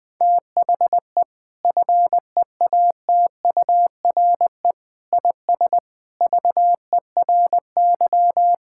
Morse code, heard as a keyed tone, about 19 wpm, 700 Hz. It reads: THE FEATURE IS VERY
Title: Sentences from Top 1000 Words - Review Part 5 - 20wpm